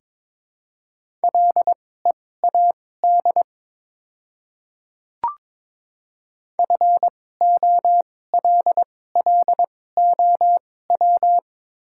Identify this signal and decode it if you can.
Morse code, heard as a keyed tone, about 22 wpm, spaced as for 16 wpm, 690 Hz. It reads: LEAD FOLLOW
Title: 2x Character Spacing - Top 500 Words - 22wpm